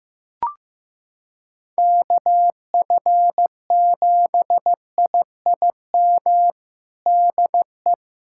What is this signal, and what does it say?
Morse code, heard as a keyed tone, about 15 wpm, 700 Hz. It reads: KF7IIM DE
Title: USA Callsign DE Callsign - 6 Characters - 15wpm